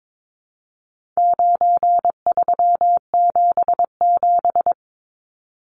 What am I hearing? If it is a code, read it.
Morse code, heard as a keyed tone, about 22 wpm, 700 Hz. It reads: 9377
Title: Numbers - 4-Digit 22wpm